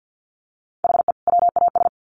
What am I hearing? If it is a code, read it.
Morse code, heard as a keyed tone, about 50 wpm, 700 Hz. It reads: HE WAS
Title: Sets of 2 Words 50wpm